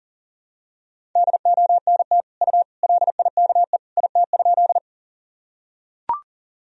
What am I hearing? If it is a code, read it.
Morse code, heard as a keyed tone, about 40 wpm, 700 Hz. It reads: DONT U LIKE IT?
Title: Sentences from Top 300 Words and Abbreviations 40wpm